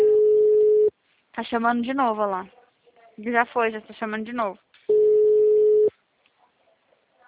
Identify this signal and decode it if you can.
Ringback tone (Europe, 425 Hz), with speech.
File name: out- -2057- - - .17644